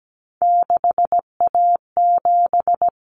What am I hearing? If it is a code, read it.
Morse code, heard as a keyed tone, about 17 wpm, 700 Hz. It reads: 6A7
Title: Common Cut Numbers - 3-Digit 17wpm